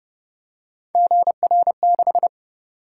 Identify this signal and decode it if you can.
Morse code, heard as a keyed tone, about 28 wpm, 700 Hz. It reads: GR6